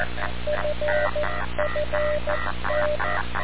Morse code, heard as a keyed tone, about 14 wpm, 600 Hz, with noise. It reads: FFT
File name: radiocap / REC z